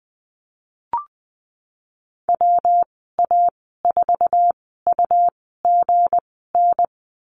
Morse code, heard as a keyed tone, about 20 wpm, spaced as for 15 wpm, 700 Hz. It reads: WA4UGN
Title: 173 Lesson 28g - Number 9 - Call Signs 20wpm